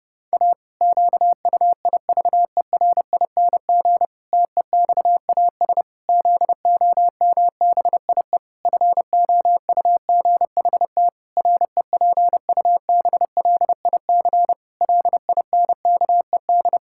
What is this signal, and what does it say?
Morse code, heard as a keyed tone, about 30 wpm, 700 Hz. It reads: A QUIVERING TEXAS ZOMBIE FOUGHT REPUBLIC LINKED
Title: Pangram Challenge v2 - 30wpm